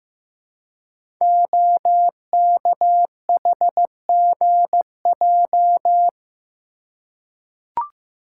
Morse code, heard as a keyed tone, about 15 wpm, 700 Hz. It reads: OKHGJ
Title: Code Groups - 5 Letter:Numbers - 15wpm